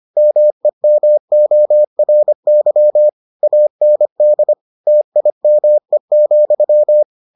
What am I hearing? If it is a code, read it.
Morse code, heard as a keyed tone, about 25 wpm, 600 Hz. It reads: MEMORY AND TIME,